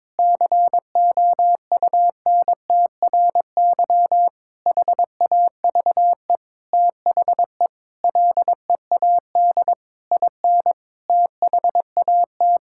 Morse code, heard as a keyed tone, about 22 wpm, 700 Hz. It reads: COUNTRY HAVE THE LEAD IN THAT
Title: Sentences from Top 1000 Words - Review Part 5 - 22wpm